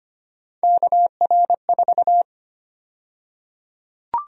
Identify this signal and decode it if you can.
Morse code, heard as a keyed tone, about 25 wpm, 700 Hz. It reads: KR4
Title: USA Call Sign Prefixes - Repeated 2x - 25wpm